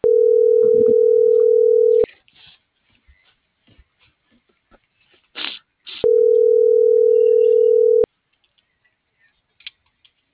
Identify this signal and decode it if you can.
Ringback tone (North America), 440 + 480 Hz.